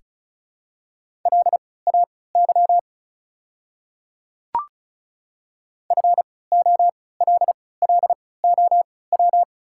Morse code, heard as a keyed tone, about 36 wpm, spaced as for 20 wpm, 690 Hz. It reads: LAY FOLLOW
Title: 3x Character Spacing - Top 500 Words - 35wpm